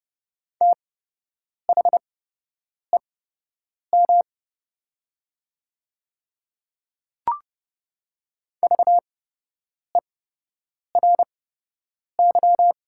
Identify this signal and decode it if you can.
Morse code, heard as a keyed tone, about 30 wpm, spaced as for 8 wpm, 700 Hz. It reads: THEM VERY
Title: 8x Character Spacing - Top 200 Words - 30wpm